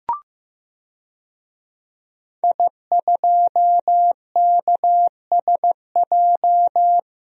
Morse code, heard as a keyed tone, about 15 wpm, 700 Hz. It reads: I2KSJ
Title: DXCC-100 Call Signs - 5 Characters - 15wpm